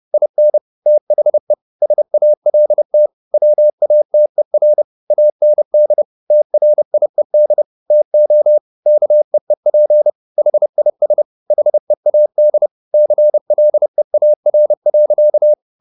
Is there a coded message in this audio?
Yes: Morse code from 0.1 to 15.5 s, heard as a keyed tone, about 30 wpm, 600 Hz. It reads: IN THE SALT WATER AND TRIED TO KEEP HIS HEAD CLEAR.